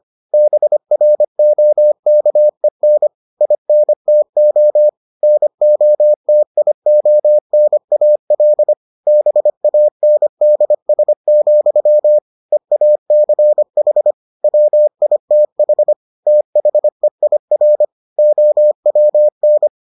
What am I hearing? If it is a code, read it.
Morse code, heard as a keyed tone, about 25 wpm, 600 Hz. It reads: BROKEN INTO NOTIONAL BANDS, EACH WITH THEIR OWN